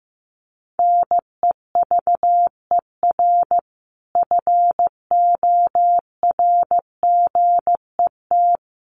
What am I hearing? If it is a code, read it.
Morse code, heard as a keyed tone, about 15 wpm, 700 Hz. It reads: NEVER FORGET
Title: Sentences from Top 1200 Words 15wpm